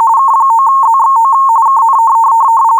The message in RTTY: K9ZXO KN